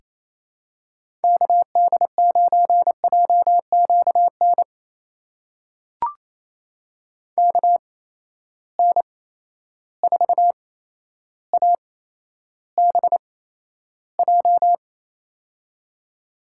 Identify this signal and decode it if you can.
Morse code, heard as a keyed tone, about 28 wpm, spaced as for 12 wpm, 700 Hz. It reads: KD9JQN KN4ABJ
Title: USA Call Signs - 6 Characters - 8x Character Spacing - 28wpm